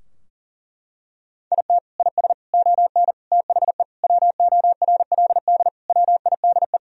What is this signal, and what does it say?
Morse code, heard as a keyed tone, about 37 wpm, 700 Hz. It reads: IT IS ON THE WORLD WIDE
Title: Sentences from Top 1000 Words - Review Part 4 - 40wpm